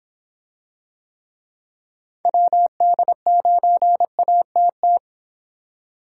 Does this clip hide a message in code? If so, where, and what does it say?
Yes: Morse code from 2.3 to 5.0 s, heard as a keyed tone, about 26 wpm, 700 Hz. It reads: WD9ATT